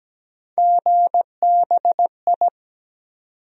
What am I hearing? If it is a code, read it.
Morse code, heard as a keyed tone, about 17 wpm, 700 Hz. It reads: GBI
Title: USA Call Sign Postfixes - 3 Letters 17wpm